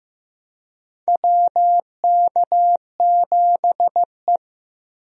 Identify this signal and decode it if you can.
Morse code, heard as a keyed tone, about 15 wpm, 700 Hz. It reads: WK7E
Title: USA Call Signs - 4 Characters - Repeated 3x - 15wpm